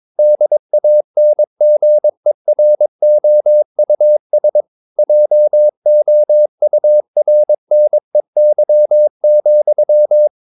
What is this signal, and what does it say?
Morse code, heard as a keyed tone, about 22 wpm, 600 Hz. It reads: DANGEROUS JOURNEY,
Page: Romanen Never af Ken Follet på engelsk.